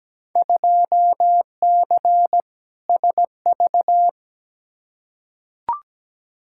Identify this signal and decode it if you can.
Morse code, heard as a keyed tone, about 17 wpm, 700 Hz. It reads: 2C SV
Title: Field Day Exchanges Encore - 17wpm